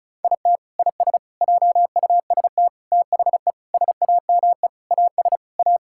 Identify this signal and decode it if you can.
Morse code, heard as a keyed tone, about 34 wpm, 700 Hz. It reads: IT IS JUST THE SAME AS A
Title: Sentences from Top 1000 Words - Review Part 6 - 35wpm